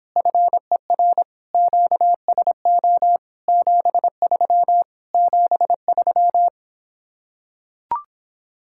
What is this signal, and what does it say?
Morse code, heard as a keyed tone, about 26 wpm, 700 Hz. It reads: FER QSO 73 73